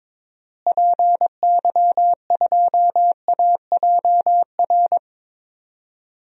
Morse code, heard as a keyed tone, about 22 wpm, 700 Hz. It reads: PY2AJR
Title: DXCC-100 Call Signs - 6 Characters - Repeated 3x - 22wpm